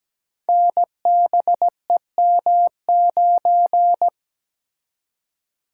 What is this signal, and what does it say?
Morse code, heard as a keyed tone, about 17 wpm, 700 Hz. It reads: NBEM9
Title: Code Groups - 5 Letter:Numbers - 17wpm